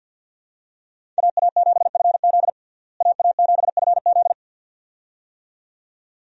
Morse code, heard as a keyed tone, about 51 wpm, 700 Hz. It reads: AA7FZ AA7FZ
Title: USA Call Signs - 5 Characters - Repeated 2x - 50wpm